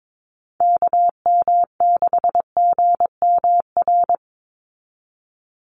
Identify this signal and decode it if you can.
Morse code, heard as a keyed tone, about 22 wpm, 700 Hz. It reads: KM6GMR
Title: USA Call Signs 22wpm